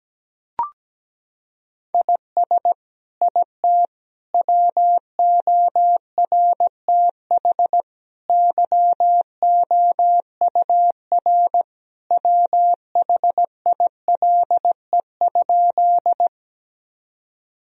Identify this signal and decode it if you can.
Morse code, heard as a keyed tone, about 17 wpm, 700 Hz. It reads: IS IT WORTH YOUR WHILE?
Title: Sentences from Top 1000 Words - Review Part 1 - 17wpm